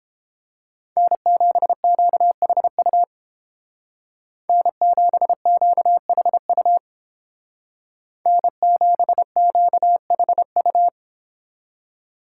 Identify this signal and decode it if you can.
Morse code, heard as a keyed tone, about 29 wpm, 700 Hz. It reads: N7QHU N7QHU N7QHU